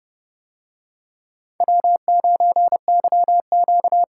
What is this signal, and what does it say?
Morse code, heard as a keyed tone, about 29 wpm, 700 Hz. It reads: W9YQ